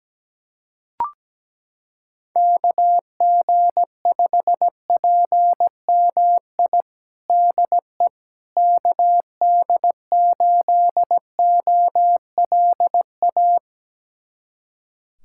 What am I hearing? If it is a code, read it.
Morse code, heard as a keyed tone, about 17 wpm, 700 Hz. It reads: KG5PMI DE KD8OLA
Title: USA Callsign DE Callsign - 6 Characters - 17wpm